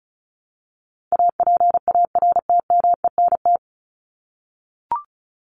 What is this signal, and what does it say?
Morse code, heard as a keyed tone, about 34 wpm, 700 Hz. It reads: APARTMENT
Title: Top 751-1000 Words 35wpm